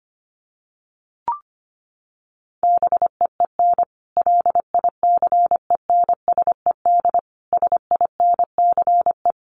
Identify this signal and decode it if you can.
Morse code, heard as a keyed tone, about 25 wpm, 700 Hz. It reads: BEEN LICENSED SINCE
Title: Common QSO Phrases 25wpm